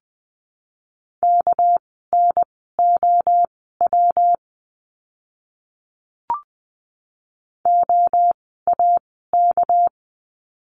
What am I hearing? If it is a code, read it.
Morse code, heard as a keyed tone, about 20 wpm, spaced as for 14 wpm, 700 Hz. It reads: KNOW OAK
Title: 199 Lesson 32f - Letter k - Words 20wpm